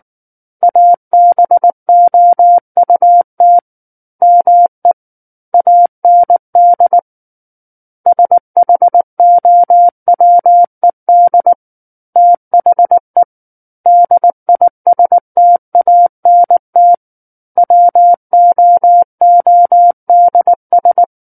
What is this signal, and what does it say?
Morse code, heard as a keyed tone, about 19 wpm, 700 Hz. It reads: ABOUT ME AND SHOWED THE DISTANT WOODS
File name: Book One / 10-Chapter 10 - 19 WPM